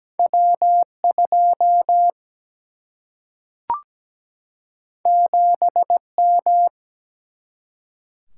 Morse code, heard as a keyed tone, about 17 wpm, 700 Hz. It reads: W 2 7 M
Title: Single-Letter Single-Number 17wpm